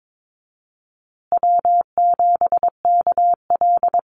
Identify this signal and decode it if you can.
Morse code, heard as a keyed tone, about 22 wpm, 700 Hz. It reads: W7KL
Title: USA Call Signs Repeated 2x 22wpm